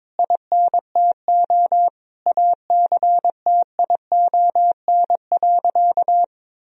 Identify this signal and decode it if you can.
Morse code, heard as a keyed tone, about 21 wpm, 700 Hz. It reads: INTO ACTION.
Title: Sentences from Top 1000 Words - Review Part 4 - 22wpm